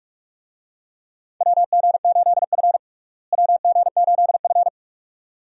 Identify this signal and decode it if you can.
Morse code, heard as a keyed tone, about 47 wpm, 700 Hz. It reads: WG8F WG8F
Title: USA Call Signs - 4 Characters - Repeated 2x - 45wpm